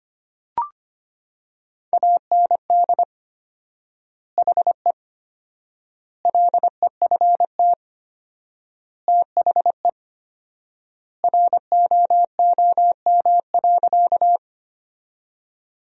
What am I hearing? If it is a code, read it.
Morse code, heard as a keyed tone, about 25 wpm, 700 Hz. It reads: AND HE LEFT THE ROOM.
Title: 4x Word-Spacing - Sentences from Top 300 Words - 25wpm